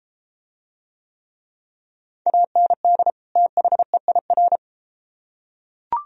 Morse code, heard as a keyed tone, about 33 wpm, 700 Hz. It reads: AND THEIR